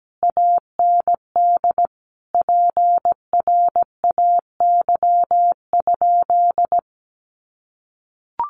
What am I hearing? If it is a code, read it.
Morse code, heard as a keyed tone, about 17 wpm, 700 Hz. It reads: AND PRAY?
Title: A Sound Of Thunder 17wpm